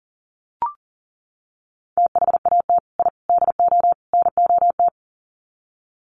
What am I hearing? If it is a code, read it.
Morse code, heard as a keyed tone, about 40 wpm, 700 Hz. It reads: THAT I DO NOT
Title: Sets of 4 Words 40wpm